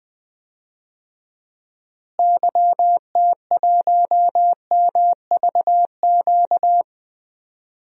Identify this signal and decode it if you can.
Morse code, heard as a keyed tone, about 20 wpm, 700 Hz. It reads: YT1MVQ
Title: DXCC-100 Call Signs - 6 Characters - Repeated 3x - 20wpm